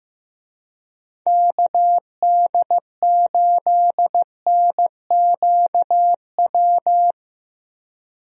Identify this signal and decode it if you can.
Morse code, heard as a keyed tone, about 15 wpm, 700 Hz. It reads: KD8NQW
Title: USA Call Signs - 6 Characters - 15wpm